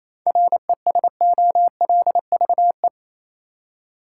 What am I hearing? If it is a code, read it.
Morse code, heard as a keyed tone, about 28 wpm, 700 Hz. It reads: RESOLVE
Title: Seven Letter Words - 28wpm